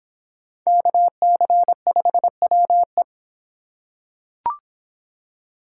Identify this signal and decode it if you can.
Morse code, heard as a keyed tone, about 25 wpm, 700 Hz. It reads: KC5WE